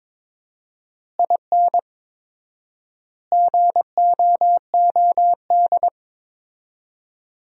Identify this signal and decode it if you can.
Morse code, heard as a keyed tone, about 22 wpm, 700 Hz. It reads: IN GOOD
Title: 4x Word-Spacing - Sentences from Top 200 Words - 22wpm